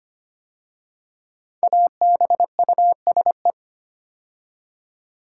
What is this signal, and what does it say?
Morse code, heard as a keyed tone, about 25 wpm, 700 Hz. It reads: ABUSE
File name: 5 Letter Words - First 3 Letters Given - 25wpm